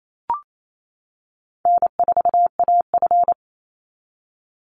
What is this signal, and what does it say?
Morse code, heard as a keyed tone, about 28 wpm, 700 Hz. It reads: N4AF
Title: Easy Call Signs - 28wpm